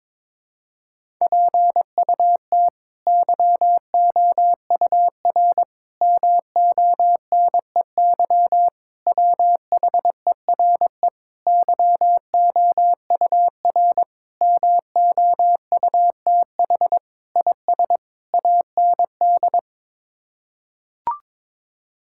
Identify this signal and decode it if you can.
Morse code, heard as a keyed tone, about 22 wpm, 700 Hz. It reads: PUT YOUR MONEY WHERE YOUR MOUTH IS AND
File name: practice / 22wpm / Sets of 8 Words - 22wpm